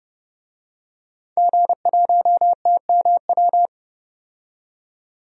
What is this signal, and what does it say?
Morse code, heard as a keyed tone, about 32 wpm, 700 Hz. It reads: G1TMW
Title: DXCC-100 Call Signs - 5 Characters - 30wpm